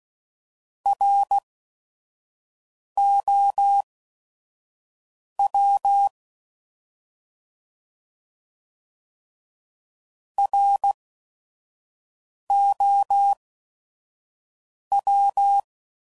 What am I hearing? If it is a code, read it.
Morse code, heard as a keyed tone, about 16 wpm, spaced as for 5 wpm, 800 Hz. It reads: ROW ROW
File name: Prefixes / By First Letter / Rs